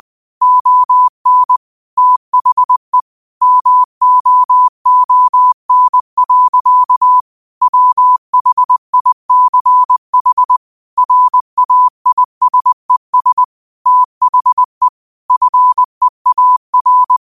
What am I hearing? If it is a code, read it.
Morse code, heard as a keyed tone, about 20 wpm, 1000 Hz. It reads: ON THE MOON. WHICH RAISES THE FEAR
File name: qotdfiles / 20wpm